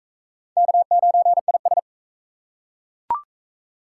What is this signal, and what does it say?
Morse code, heard as a keyed tone, about 37 wpm, 700 Hz. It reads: K9IS